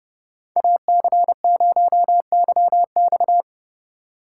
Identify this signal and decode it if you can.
Morse code, heard as a keyed tone, about 35 wpm, 700 Hz. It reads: AC0YX
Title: Speed Racing - USA 2x2 Call Signs 30wpm